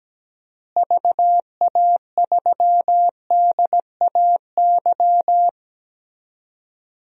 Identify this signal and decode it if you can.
Morse code, heard as a keyed tone, about 17 wpm, 700 Hz. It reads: VA3DAY
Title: DXCC-100 Call Signs - 6 Characters - Repeated 3x - 17wpm